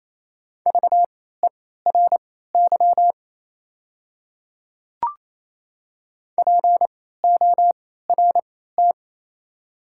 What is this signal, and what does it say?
Morse code, heard as a keyed tone, about 28 wpm, spaced as for 16 wpm, 700 Hz. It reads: VERY PORT
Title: 3x Character Spacing - Top 300 Words - 28wpm